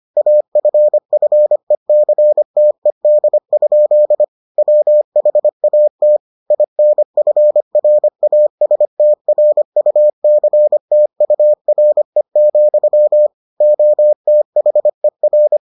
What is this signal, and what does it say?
Morse code, heard as a keyed tone, about 25 wpm, 600 Hz. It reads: AFFECTED? WHAT INFRASTRUCTURE, OTHER